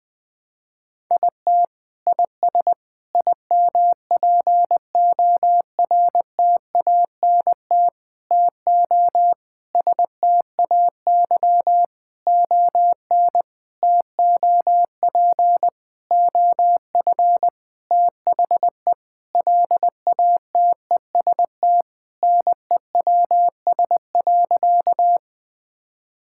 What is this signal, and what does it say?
Morse code, heard as a keyed tone, about 20 wpm, 700 Hz. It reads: IT IS IMPORTANT TO STAY ON TOP OF THE LATEST NEWS.
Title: Sentences with 6-Word Phrases - 20wpm